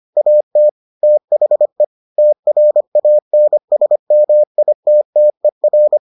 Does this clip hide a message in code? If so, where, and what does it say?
Yes: Morse code from 0.2 to 6.0 s, heard as a keyed tone, about 25 wpm, 600 Hz. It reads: AT THE TRANSMITTER